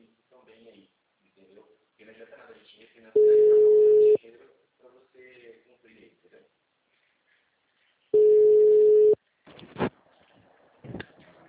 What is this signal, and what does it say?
Ringback tone (Europe, 425 Hz), with speech.
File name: out- -2005- - - .103